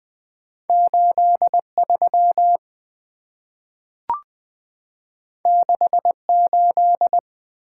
Morse code, heard as a keyed tone, about 20 wpm, 690 Hz. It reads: 83 68
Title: Two Letter-Number Combinations - 20wpm